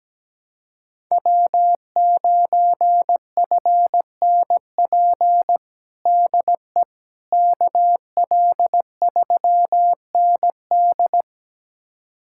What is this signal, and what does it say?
Morse code, heard as a keyed tone, about 17 wpm, 700 Hz. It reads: W9FNP DE KL3ND
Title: USA Callsign DE Callsign - 5 Characters - 17wpm